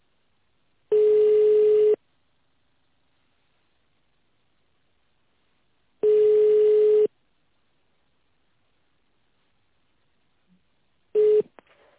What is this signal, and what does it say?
Ringback tone (Europe, 425 Hz).